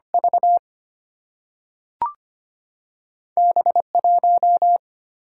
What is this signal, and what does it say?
Morse code, heard as a keyed tone, about 25 wpm, 700 Hz. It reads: V B 1
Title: Single-Letter Single-Number 25wpm